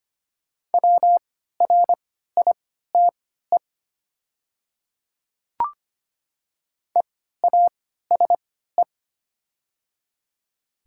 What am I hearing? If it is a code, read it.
Morse code, heard as a keyed tone, about 25 wpm, spaced as for 14 wpm, 690 Hz. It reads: WRITE EASE
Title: 3x Character Spacing - Top 500 Words - 25wpm